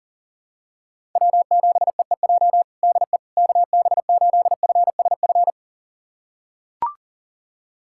Morse code, heard as a keyed tone, about 40 wpm, 700 Hz. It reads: W7EEJ DE KB8FSF
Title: USA Callsign DE Callsign 40wpm